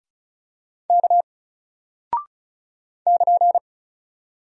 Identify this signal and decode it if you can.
Morse code, heard as a keyed tone, about 36 wpm, 700 Hz. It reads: K <KN>
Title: Single Character (Letter:Number:Punctuation:Prosign) Alt 35wpm